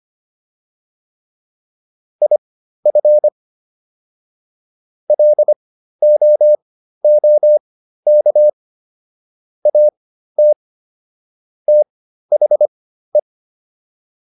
Morse code, heard as a keyed tone, about 25 wpm, spaced as for 13 wpm, 600 Hz. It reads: IF LOOK AT THE